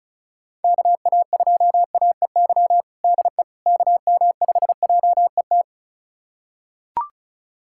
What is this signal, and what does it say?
Morse code, heard as a keyed tone, about 36 wpm, 690 Hz. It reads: KA2AEY DE KM5JET
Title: USA Callsign DE Callsign - 6 Characters - 35wpm